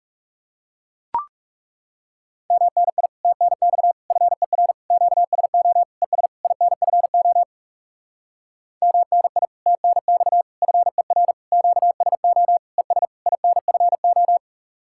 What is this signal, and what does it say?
Morse code, heard as a keyed tone, about 40 wpm, 700 Hz. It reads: MNI TNX FER QSO ES INFO MNI TNX FER QSO ES INFO